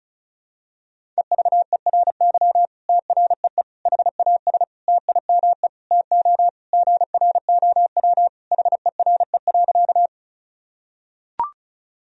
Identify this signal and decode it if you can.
Morse code, heard as a keyed tone, about 36 wpm, 700 Hz. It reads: EVERY TREE HAS TIME TO GROW HERE.
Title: Sentences from Top 1000 Words - Review Part 1 - 35wpm